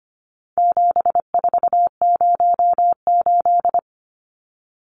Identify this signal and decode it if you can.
Morse code, heard as a keyed tone, about 25 wpm, 700 Hz. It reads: 7408
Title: Numbers - 4-Digit 25wpm